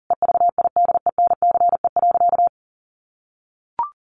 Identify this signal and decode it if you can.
Morse code, heard as a keyed tone, about 40 wpm, 700 Hz. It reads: EVIDENCE.
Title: Sentences from Top 1300 Words 40wpm